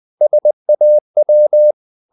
Morse code, heard as a keyed tone, about 20 wpm, 600 Hz. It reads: SAW
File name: the old man and the sea part 2 20wpm